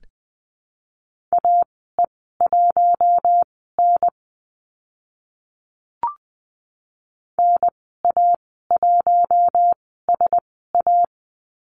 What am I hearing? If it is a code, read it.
Morse code, heard as a keyed tone, about 20 wpm, spaced as for 14 wpm, 700 Hz. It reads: AE1N NA1SA
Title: 042 Lesson 8g - Number 1 - Call Signs 20wpm